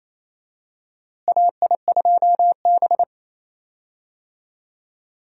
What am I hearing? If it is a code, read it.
Morse code, heard as a keyed tone, about 28 wpm, 700 Hz. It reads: AI2B
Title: USA Call Signs - 4 Characters - Repeated 2x - 28wpm